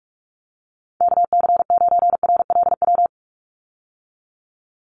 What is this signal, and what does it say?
Morse code, heard as a keyed tone, about 44 wpm, 700 Hz. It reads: KC9RLW
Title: USA Call Signs 45wpm